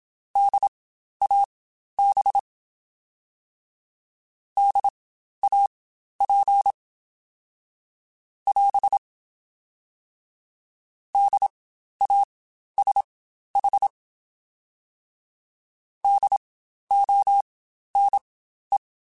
Morse code, heard as a keyed tone, about 27 wpm, spaced as for 12 wpm, 800 Hz. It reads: DAB DAP <AS> DASH DONE
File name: Spaced Alphas / Spaced Alphas mp3s / Ds sp @25